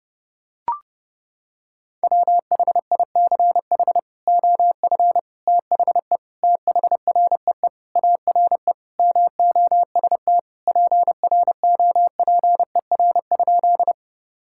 Morse code, heard as a keyed tone, about 30 wpm, 700 Hz. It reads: WHICH OF THE THREE ARE MOST PROPER?
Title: Sentences from Top 1000 Words - Review Part 5 - 30wpm